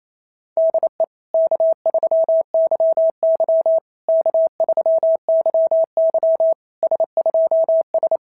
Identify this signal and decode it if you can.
Morse code, heard as a keyed tone, about 27 wpm, 690 Hz. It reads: DE K3YY K3YY S2S
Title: SOTA QSO Practice 28wpm